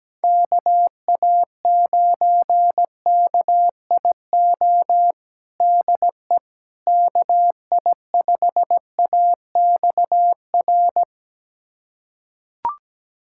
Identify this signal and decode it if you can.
Morse code, heard as a keyed tone, about 17 wpm, 700 Hz. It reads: KA9KIO DE KI5AXR
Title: USA Callsign DE Callsign - 6 Characters - 17wpm